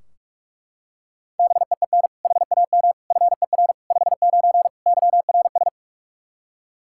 Morse code, heard as a keyed tone, about 43 wpm, 700 Hz. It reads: BEEN HAM FER 59 YRS
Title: Practice QSO - Exchange 3 - More Info - 45wpm